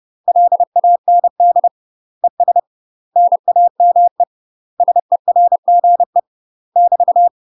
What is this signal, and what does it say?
Morse code, heard as a keyed tone, about 30 wpm, 700 Hz. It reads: LAND ES NAME SERGE <BT>
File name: UK MASTER QSOs TRAINING / 30WPM / UK MASTER QSOs 0026